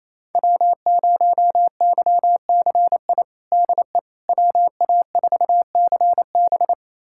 Morse code, heard as a keyed tone, about 28 wpm, 700 Hz. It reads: W0YCI DE WA4CB
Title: USA Callsign DE Callsign - 5 Characters - 28wpm